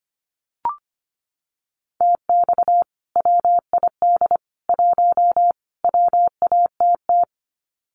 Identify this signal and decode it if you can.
Morse code, heard as a keyed tone, about 25 wpm, 700 Hz. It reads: TX WID 1 WATT
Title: Common QSO Phrases 25wpm